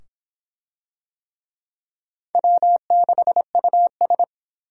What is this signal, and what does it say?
Morse code, heard as a keyed tone, about 22 wpm, 700 Hz. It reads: W6US